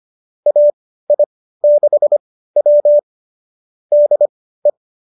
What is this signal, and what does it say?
Morse code, heard as a keyed tone, about 25 wpm, spaced as for 15 wpm, 590 Hz. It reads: AI6W DE